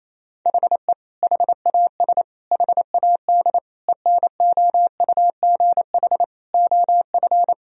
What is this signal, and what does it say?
Morse code, heard as a keyed tone, about 28 wpm, 700 Hz. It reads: HE HAS HAD ENOUGH OF
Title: Sentences from Top 1000 Words - Review Part 3 - 28wpm